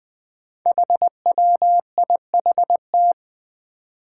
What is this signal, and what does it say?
Morse code, heard as a keyed tone, about 21 wpm, 700 Hz. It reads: HWIHT
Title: Five Challenging Letter-Number Combinations 20wpm